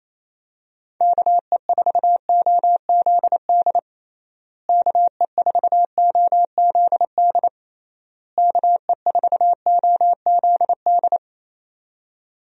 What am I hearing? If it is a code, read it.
Morse code, heard as a keyed tone, about 28 wpm, 700 Hz. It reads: KE4OZD KE4OZD KE4OZD
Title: USA Call Signs - 6 Characters - Repeated 3x - 28wpm